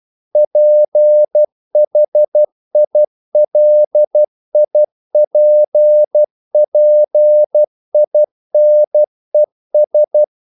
Morse code, heard as a keyed tone, about 12 wpm, 600 Hz. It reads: PHILIPPINES
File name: CW / Practice / NationalCapitals12